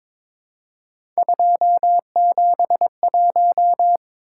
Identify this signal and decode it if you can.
Morse code, heard as a keyed tone, about 22 wpm, 700 Hz. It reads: 271
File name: practice / 22wpm / Three Letter-Number Combinations - 22wpm